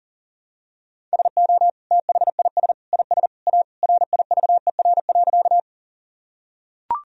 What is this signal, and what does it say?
Morse code, heard as a keyed tone, about 37 wpm, 700 Hz. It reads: SO THIS IS A RIVER.
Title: Sentences from Top 1000 Words - Review Part 1 - 40wpm